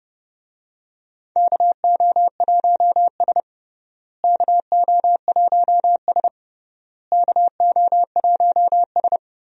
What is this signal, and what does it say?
Morse code, heard as a keyed tone, about 30 wpm, 700 Hz. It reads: KO1S KO1S KO1S
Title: USA Call Signs - 4 Characters - Repeated 3x - 30wpm